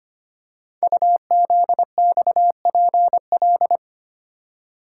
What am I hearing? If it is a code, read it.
Morse code, heard as a keyed tone, about 25 wpm, 700 Hz. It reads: UZXPL
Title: Code Groups - 5 Letters - 25wpm